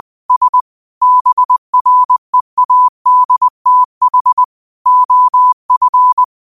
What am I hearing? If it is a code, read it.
Morse code, heard as a keyed tone, about 20 wpm, 1000 Hz. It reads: S BREADTH OF